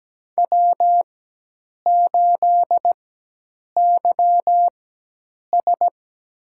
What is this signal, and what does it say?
Morse code, heard as a keyed tone, about 17 wpm, spaced as for 8 wpm, 700 Hz. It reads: W8YS
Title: USA Call Signs - 4x Character Spacing - 17wpm